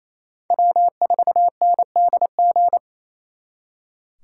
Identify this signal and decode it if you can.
Morse code, heard as a keyed tone, about 32 wpm, 700 Hz. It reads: W4NDG
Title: Speed Racing - USA 1x3 Call Signs 28wpm